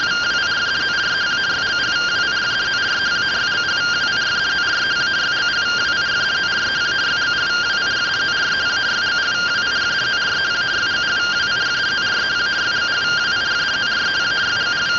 Two-tone teleprinter signal, about 170 Hz shift.
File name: findings / CT 1 Scout 0001.gen-41 rtty rx